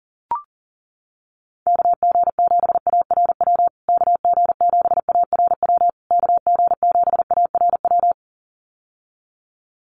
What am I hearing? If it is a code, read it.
Morse code, heard as a keyed tone, about 40 wpm, 700 Hz. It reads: KG7ARW KG7ARW KG7ARW
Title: USA Call Signs Repeated 3x 40wpm